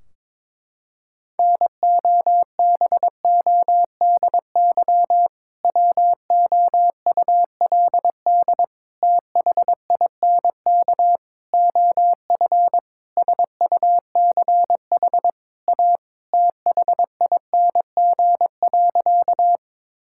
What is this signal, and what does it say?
Morse code, heard as a keyed tone, about 22 wpm, 700 Hz. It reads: NOBODY WOULD THINK OF SUCH A THING.
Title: Sentences from Top 1600 Words 22wpm